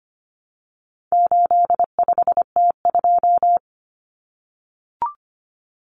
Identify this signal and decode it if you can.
Morse code, heard as a keyed tone, about 25 wpm, 700 Hz. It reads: 85T2
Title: Common Cut Numbers - 4-Digit 25wpm